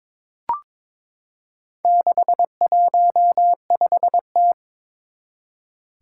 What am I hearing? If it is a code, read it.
Morse code, heard as a keyed tone, about 22 wpm, 700 Hz. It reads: 615T
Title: Four Challenging Letter-Number Combinations 22wpm